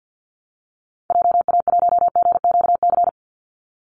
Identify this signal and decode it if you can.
Morse code, heard as a keyed tone, about 51 wpm, 700 Hz. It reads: JA1GQC
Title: Call Signs - 50wpm